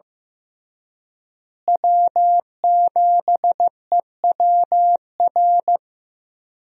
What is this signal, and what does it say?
Morse code, heard as a keyed tone, about 15 wpm, 700 Hz. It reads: W7EWR
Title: USA Call Signs - 5 Characters - Repeated 2x - 15wpm